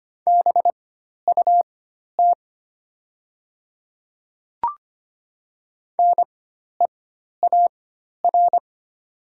Morse code, heard as a keyed tone, about 25 wpm, spaced as for 12 wpm, 690 Hz. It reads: BUT NEAR
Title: 4x Character Spacing - Top 200 Words - 25wpm